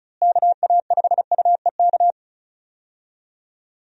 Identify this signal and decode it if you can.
Morse code, heard as a keyed tone, about 36 wpm, 700 Hz. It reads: KA5UEK
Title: USA Call Signs - 6 Characters - Repeated 3x - 35wpm